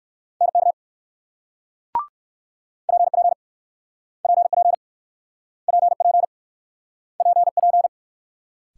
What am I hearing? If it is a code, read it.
Morse code, heard as a keyed tone, about 55 wpm, 690 Hz. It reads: NK PP PP PP PP
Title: Speed-Racing Multi-letter Phonemes - 45wpm